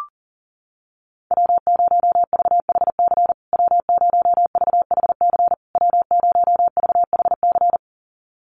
Morse code, heard as a keyed tone, about 40 wpm, 700 Hz. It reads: W0VHC W0VHC W0VHC
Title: USA Call Signs Repeated 3x 40wpm